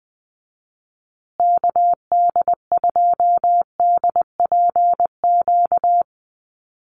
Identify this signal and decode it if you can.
Morse code, heard as a keyed tone, about 20 wpm, 700 Hz. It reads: KD2DPQ
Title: USA Call Signs Repeated 2x 20wpm